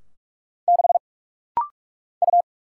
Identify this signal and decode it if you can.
Morse code, heard as a keyed tone, about 46 wpm, 700 Hz. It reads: 6 U
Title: Single Letter-Number Alt - Rapid-Fire 45wpm